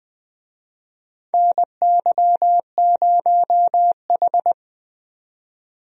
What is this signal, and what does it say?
Morse code, heard as a keyed tone, about 20 wpm, 700 Hz. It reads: NY0H
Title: USA Call Signs - 4 Characters - 20wpm